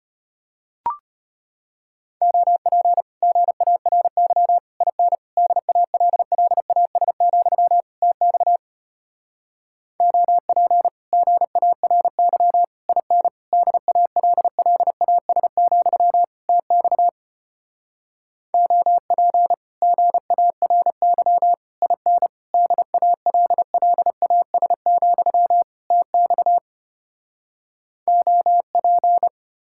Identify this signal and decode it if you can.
Morse code, heard as a keyed tone, about 33 wpm, 700 Hz. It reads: OP GARY IN DALLAS, TX OP GARY IN DALLAS, TX OP GARY IN DALLAS, TX OP